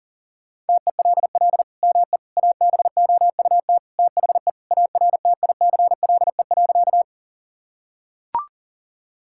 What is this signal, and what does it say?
Morse code, heard as a keyed tone, about 40 wpm, 690 Hz. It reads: TELL ME ABOUT THE ARTICLE.
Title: Sentences from Top 1000 Words - Review Part 5 - 40wpm